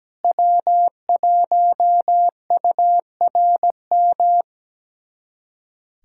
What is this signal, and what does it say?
Morse code, heard as a keyed tone, about 17 wpm, 700 Hz. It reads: W1URM
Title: USA Callsign DE Callsign - 5 Characters - 17wpm